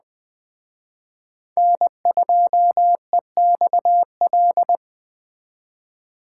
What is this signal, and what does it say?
Morse code, heard as a keyed tone, about 20 wpm, 700 Hz. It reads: N2EXL
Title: Speed Racing - USA 1x3 Call Signs 17wpm